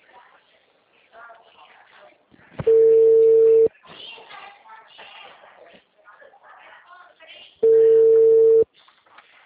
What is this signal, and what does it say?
Ringback tone (China, 450 Hz), with speech.